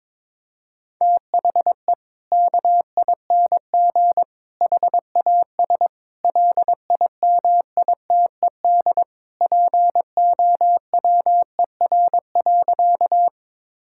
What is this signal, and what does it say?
Morse code, heard as a keyed tone, about 22 wpm, 700 Hz. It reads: THE KING HAS LIMITED POWER.
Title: Sentences from Top 1000 Words - Review Part 6 - 22wpm